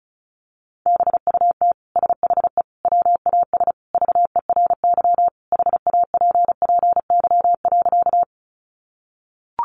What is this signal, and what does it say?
Morse code, heard as a keyed tone, about 34 wpm, 700 Hz. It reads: BUT SHE WAS VERY HAPPY.
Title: Sentences from Top 1000 Words 35wpm